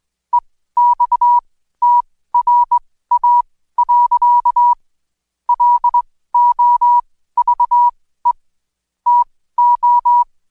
Morse code, heard as a keyed tone, about 21 wpm, spaced as for 15 wpm, 970 Hz. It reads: EXTRA. LOVE TO
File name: Code Quick / AudioDownld2 / Seg123-Day28-DayDrill 28